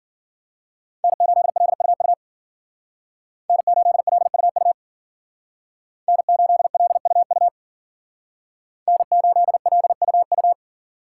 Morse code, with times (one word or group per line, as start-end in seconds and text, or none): 1.0-2.1 N8LUU
3.5-4.7 N8LUU
6.1-7.5 N8LUU
8.9-10.5 N8LUU